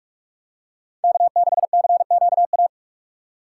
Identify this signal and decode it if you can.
Morse code, heard as a keyed tone, about 44 wpm, 700 Hz. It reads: K6CQA